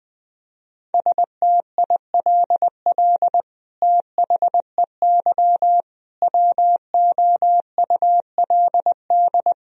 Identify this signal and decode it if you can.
Morse code, heard as a keyed tone, about 20 wpm, 700 Hz. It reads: STILL THEY WOULD
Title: Sentences from Top 1000 Words - Review Part 1 - 20wpm